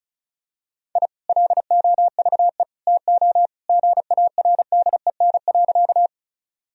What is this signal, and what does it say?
Morse code, heard as a keyed tone, about 36 wpm, 700 Hz. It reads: I LOVE TO GARDEN.
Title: Sentences from Top 1000 Words - Review Part 4 - 35wpm